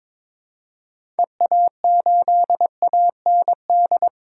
Morse code, heard as a keyed tone, about 22 wpm, 700 Hz. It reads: EA8AND
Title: DXCC-100 Call Signs - 6 Characters - Repeated 2x - 22wpm